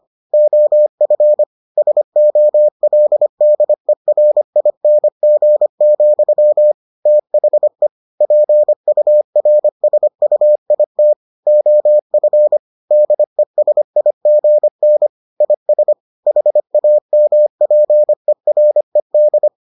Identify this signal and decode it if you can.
Morse code, heard as a keyed tone, about 25 wpm, 600 Hz. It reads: OF SOLDERING, THE PURSUIT OF DESIGN IS HAMPERED